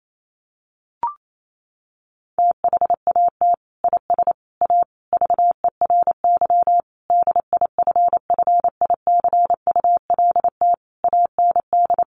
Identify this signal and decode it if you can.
Morse code, heard as a keyed tone, about 28 wpm, 700 Hz. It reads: THAT IS A VERY DIFFICULT AND
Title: Sentences from Top 1200 Words 28wpm